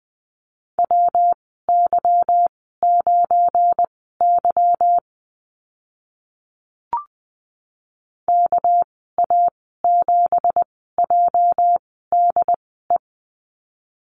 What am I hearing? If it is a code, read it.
Morse code, heard as a keyed tone, about 20 wpm, spaced as for 15 wpm, 700 Hz. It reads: WY9Y KA7JDE
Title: 231 Lesson 37g - Letter x - Call Signs 20wpm